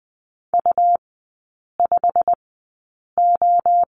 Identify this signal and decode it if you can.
Morse code, heard as a keyed tone, about 20 wpm, spaced as for 8 wpm, 700 Hz. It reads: U5O
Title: 100 Lesson 17e - Letter u - Three Characters 20wpm